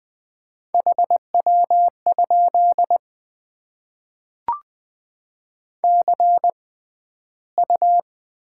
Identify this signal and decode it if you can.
Morse code, heard as a keyed tone, about 20 wpm, spaced as for 12 wpm, 700 Hz. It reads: HW? CU
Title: Top 20 QSO Elements - 6x Character Spacing - 20wpm